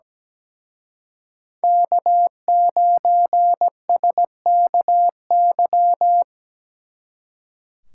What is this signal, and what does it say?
Morse code, heard as a keyed tone, about 20 wpm, 700 Hz. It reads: K9SKY
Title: Speed Racing - USA 1x3 Call Signs 17wpm